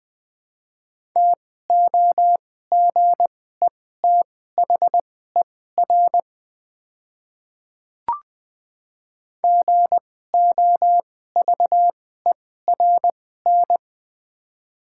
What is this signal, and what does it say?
Morse code, heard as a keyed tone, about 20 wpm, spaced as for 14 wpm, 700 Hz. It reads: TOGETHER GOVERN
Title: 2x Character Spacing - Top 500 Words - 20wpm